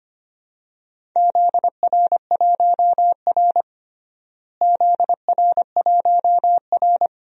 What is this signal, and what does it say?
Morse code, heard as a keyed tone, about 25 wpm, 700 Hz. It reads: ZR1R ZR1R
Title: DXCC-100 Call Signs - 4 Characters - Repeated 3x - 25wpm